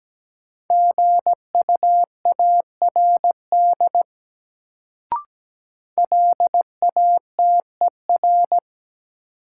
Morse code, heard as a keyed tone, about 17 wpm, 700 Hz. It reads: GUARD LATER
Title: Sentences from Top 700 Words - New Word Drill - 17wpm